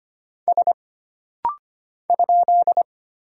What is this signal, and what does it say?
Morse code, heard as a keyed tone, about 25 wpm, 700 Hz. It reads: S ?
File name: Single Character (Letter:Number:Punctuation:Prosign) - Rapid-Fire - 25wpm